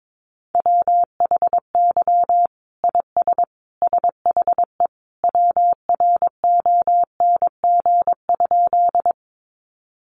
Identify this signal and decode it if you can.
Morse code, heard as a keyed tone, about 22 wpm, 700 Hz. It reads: WHY IS SHE WRONG?
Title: Sentences from Top 900 Words 22wpm